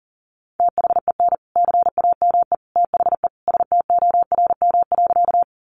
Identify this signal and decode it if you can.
Morse code, heard as a keyed tone, about 40 wpm, 700 Hz. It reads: THEN CAME THE STORM.
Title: Sentences from Top 1500 Words 40wpm